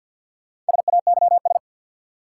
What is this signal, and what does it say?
Morse code, heard as a keyed tone, about 50 wpm, 700 Hz. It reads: SAYS
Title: Sets of 2 Words - New Word Drill - 50wpm